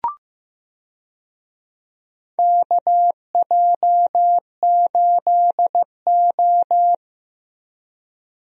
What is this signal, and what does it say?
Morse code, heard as a keyed tone, about 15 wpm, 700 Hz. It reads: KJ8O
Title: USA Call Signs - 4 Characters - Repeated 2x - 15wpm